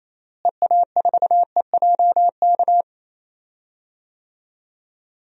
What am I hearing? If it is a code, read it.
Morse code, heard as a keyed tone, about 28 wpm, 700 Hz. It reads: EA4EJK
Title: DXCC-100 Call Signs - 6 Characters - Repeated 2x - 28wpm